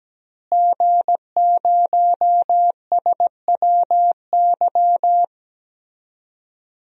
Morse code, heard as a keyed tone, about 17 wpm, 700 Hz. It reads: G0SWY
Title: DXCC-100 Call Signs - 5 Characters - Repeated 2x - 17wpm